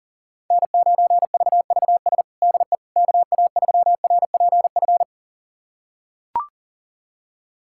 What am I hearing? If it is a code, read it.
Morse code, heard as a keyed tone, about 37 wpm, 700 Hz. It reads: N9VVS DE KA3RPF
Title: USA Callsign DE Callsign 40wpm